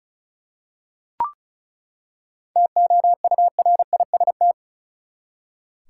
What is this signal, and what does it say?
Morse code, heard as a keyed tone, about 34 wpm, 700 Hz. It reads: TOURIST
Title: Seven Letter Words - 35wpm